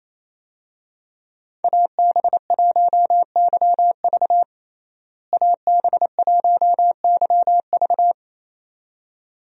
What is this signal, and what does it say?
Morse code, heard as a keyed tone, about 27 wpm, 700 Hz. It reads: AB1YV AB1YV
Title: USA Call Signs - 5 Characters - Repeated 2x - 28wpm